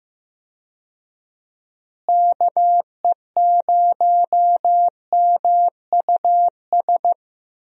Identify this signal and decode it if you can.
Morse code, heard as a keyed tone, about 15 wpm, 700 Hz. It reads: KE0MUS
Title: USA Call Signs - 6 Characters - Repeated 3x - 15wpm